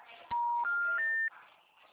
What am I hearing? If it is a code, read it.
Special information tone.
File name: out- -2012- - - .192